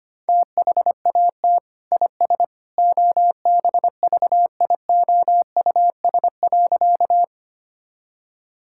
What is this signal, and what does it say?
Morse code, heard as a keyed tone, about 25 wpm, 700 Hz. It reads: THAT IS OBVIOUS.
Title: Sentences from Top 1600 Words 25wpm